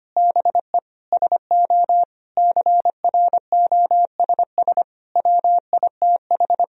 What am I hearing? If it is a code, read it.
Morse code, heard as a keyed tone, about 25 wpm, 700 Hz. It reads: BE SO CROSS WITH
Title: Sentences from Top 1000 Words - Review Part 1 - 25wpm